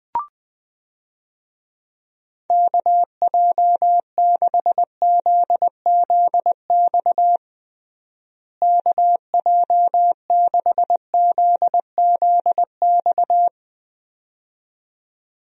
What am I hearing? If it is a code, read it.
Morse code, heard as a keyed tone, about 20 wpm, 700 Hz. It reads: KJ6ZZX KJ6ZZX
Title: USA Call Signs - 6 Characters - Repeated 2x - 20wpm